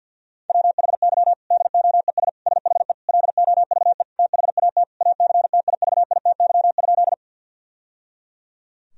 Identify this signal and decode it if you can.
Morse code, heard as a keyed tone, about 50 wpm, 700 Hz. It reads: WHY DOES SHE LOVE THAT ACTIVITY?
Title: Sentences from Top 1000 Words - Review Part 4 - 50wpm